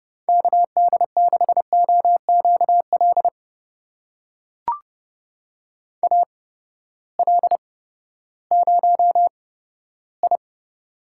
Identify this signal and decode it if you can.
Morse code, heard as a keyed tone, about 30 wpm, spaced as for 15 wpm, 700 Hz. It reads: KD6OQL AL0I
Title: USA Call Signs - 8x Character Spacing - 30wpm